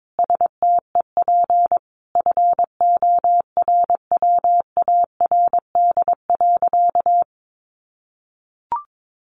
Morse code, heard as a keyed tone, about 22 wpm, 700 Hz. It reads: STEP FORWARD.
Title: Sentences from Top 1300 Words 22wpm